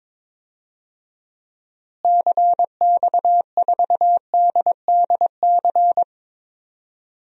Morse code, heard as a keyed tone, about 22 wpm, 700 Hz. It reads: CX4DDC
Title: DXCC-100 Call Signs - 6 Characters - Repeated 3x - 22wpm